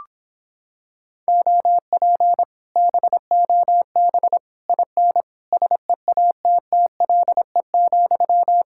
Morse code, heard as a keyed tone, about 26 wpm, 700 Hz. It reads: OP BOB IN SEATTLE,